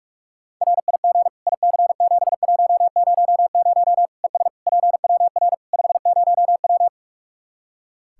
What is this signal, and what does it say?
Morse code, heard as a keyed tone, about 46 wpm, 700 Hz. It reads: RIG IC7100 ES PWR 50W
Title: Practice QSO - Exchange 2 - Info - 45wpm